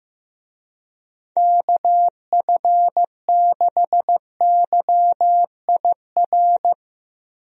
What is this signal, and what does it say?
Morse code, heard as a keyed tone, about 15 wpm, 700 Hz. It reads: KF6YIR
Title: USA Call Signs - 6 Characters - Repeated 2x - 15wpm